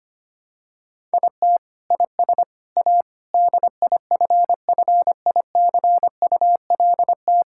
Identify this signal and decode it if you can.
Morse code, heard as a keyed tone, about 25 wpm, 700 Hz. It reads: IT IS A DIFFICULT
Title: Sentences from Top 1000 Words - Review Part 6 - 25wpm